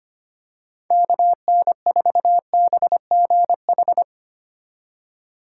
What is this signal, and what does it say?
Morse code, heard as a keyed tone, about 25 wpm, 700 Hz. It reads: KN4BGH
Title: USA Call Signs - 6 Characters - 2x Character Spacing - 25wpm